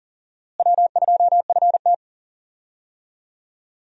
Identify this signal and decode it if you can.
Morse code, heard as a keyed tone, about 38 wpm, 700 Hz. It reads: W2FT
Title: USA Call Signs - 4 Characters - Repeated 3x - 40wpm